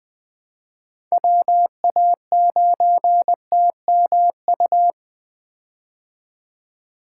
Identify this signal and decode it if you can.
Morse code, heard as a keyed tone, about 21 wpm, 700 Hz. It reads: WA9TMU
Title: 2023 POTA Hunter Call Signs - 20wpm